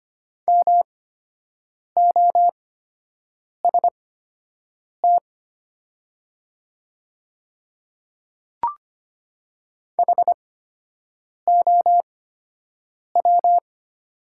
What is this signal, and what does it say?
Morse code, heard as a keyed tone, about 25 wpm, spaced as for 7 wpm, 700 Hz. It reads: MOST HOW
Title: 8x Character Spacing - Top 100 Words - 25wpm